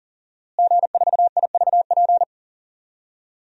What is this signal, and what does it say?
Morse code, heard as a keyed tone, about 40 wpm, 700 Hz. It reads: G4IVP
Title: DXCC-100 Call Signs - 5 Characters - Repeated 3x - 40wpm